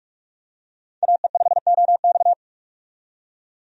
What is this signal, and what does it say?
Morse code, heard as a keyed tone, about 46 wpm, 700 Hz. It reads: AE5OX
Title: USA Call Signs - 5 Characters - 45wpm